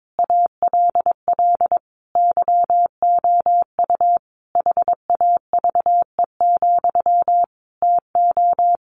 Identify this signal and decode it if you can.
Morse code, heard as a keyed tone, about 22 wpm, 700 Hz. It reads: ALL YOU HAVE, TO
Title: Sentences from Top 1100 Words 22wpm